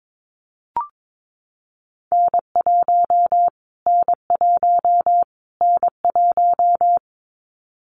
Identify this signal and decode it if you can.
Morse code, heard as a keyed tone, about 22 wpm, 700 Hz. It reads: N1 N1 N1
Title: Callsign Prefixes for Top 25 DXCC - Repeated 3x - 22wpm